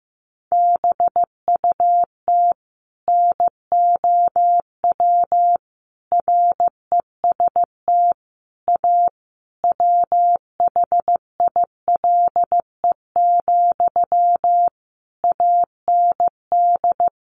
Morse code, heard as a keyed tone, about 15 wpm, 700 Hz. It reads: BUT NOW REST A WHILE, AND
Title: Sentences from Top 1100 Words 15wpm